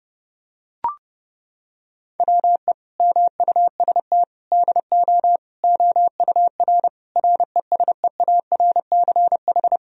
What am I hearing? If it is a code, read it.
Morse code, heard as a keyed tone, about 30 wpm, 700 Hz. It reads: WE MUST DO OUR RESEARCH
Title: Sentences from Top 1000 Words - Review Part 4 - 30wpm